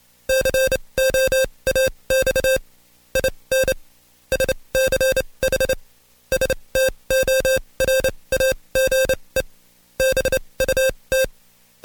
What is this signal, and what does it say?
Morse code, heard as a keyed tone, about 27 wpm, spaced as for 22 wpm, 530 Hz. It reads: COAX IN SCH STORAGE BUT
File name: Code Quick / AudioDownld2 / Seg137-Stepper10 21WPM